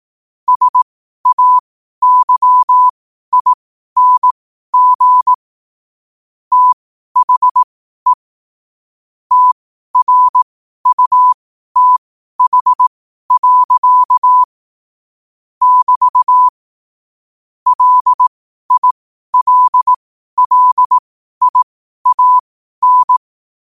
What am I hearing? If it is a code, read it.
Morse code, heard as a keyed tone, about 18 wpm, spaced as for 12 wpm, 1000 Hz. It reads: SAYING THE TRUTH. <BT> LILLIAN
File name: qotdfiles / 12wpm